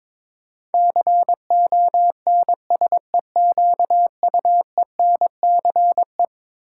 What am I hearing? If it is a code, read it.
Morse code, heard as a keyed tone, about 22 wpm, 700 Hz. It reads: CONSEQUENCE
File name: practice / 22wpm / Eleven Letter Words - 22wpm